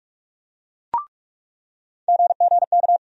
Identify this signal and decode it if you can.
Morse code, heard as a keyed tone, about 46 wpm, 700 Hz. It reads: GGK
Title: USA Call Sign Postfixes - 3 Letters 45wpm